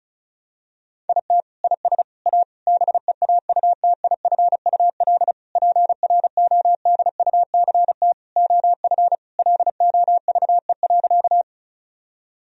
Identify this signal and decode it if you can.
Morse code, heard as a keyed tone, about 36 wpm, 700 Hz. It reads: IT IS A BEAUTIFUL PRODUCT OF LOVE.
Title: Sentences from Top 1000 Words - Review Part 2 - 35wpm